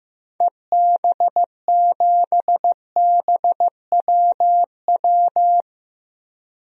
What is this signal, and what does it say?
Morse code, heard as a keyed tone, about 15 wpm, 700 Hz. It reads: EB7BWW
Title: DXCC-100 Call Signs - 6 Characters - Repeated 3x - 15wpm